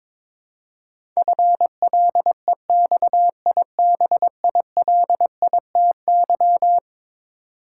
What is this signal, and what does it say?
Morse code, heard as a keyed tone, about 22 wpm, 700 Hz. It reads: FLEXIBILITY
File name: practice / 22wpm / Eleven Letter Words - 22wpm